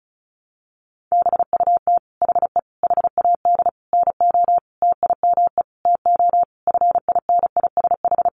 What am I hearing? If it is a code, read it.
Morse code, heard as a keyed tone, about 34 wpm, 700 Hz. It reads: BUT HE HAD NO TIME TO FINISH
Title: Sentences from Top 1200 Words 35wpm